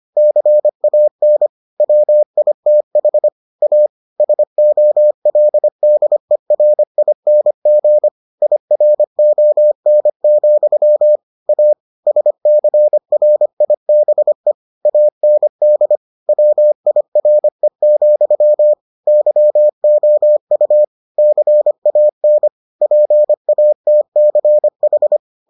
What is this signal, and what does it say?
Morse code, heard as a keyed tone, about 25 wpm, 600 Hz. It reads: CAN WITH A SOLDERING IRON, A SCRIBE AND WIRE, YOU CAN PATCH